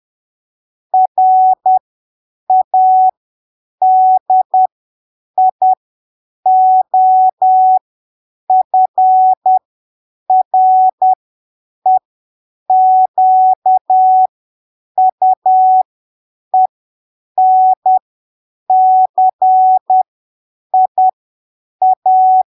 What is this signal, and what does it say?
Morse code, heard as a keyed tone, about 10 wpm, spaced as for 7 wpm, 750 Hz. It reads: RADIOFREQUENCIA
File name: arquivos / texto / texto 06 10ppm